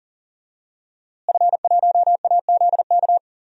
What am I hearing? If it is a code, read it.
Morse code, heard as a keyed tone, about 37 wpm, 700 Hz. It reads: F1AZK
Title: DXCC-100 Call Signs - 5 Characters - Repeated 2x - 40wpm